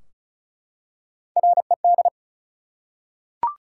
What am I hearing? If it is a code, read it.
Morse code, heard as a keyed tone, about 36 wpm, 700 Hz. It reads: RED
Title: Three Letter Words - 35wpm